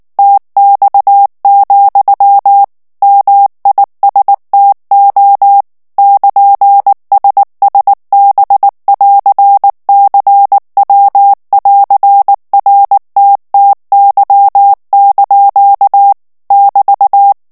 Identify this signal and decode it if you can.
Morse code, heard as a keyed tone, about 19 wpm, 800 Hz. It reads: TX, MISTO <KN>SSB<AR>CW<AR>RTTY) <BT>
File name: morse mp3 / ITALIEN texte clair 2 vitesse 18 mots minute